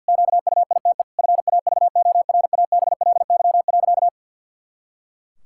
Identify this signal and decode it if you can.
Morse code, heard as a keyed tone, about 50 wpm, 700 Hz. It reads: QUITE FAVORABLY.
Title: Pangram Challenge v2 - 50wpm